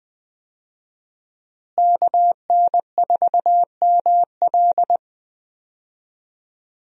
Morse code, heard as a keyed tone, about 20 wpm, 700 Hz. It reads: KN4ML
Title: USA Call Signs - 5 Characters - Repeated 2x - 20wpm